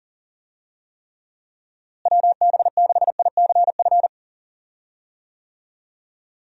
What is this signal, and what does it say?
Morse code, heard as a keyed tone, about 37 wpm, 700 Hz. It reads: WB6ICF
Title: USA Call Signs - 6 Characters - Repeated 2x - 40wpm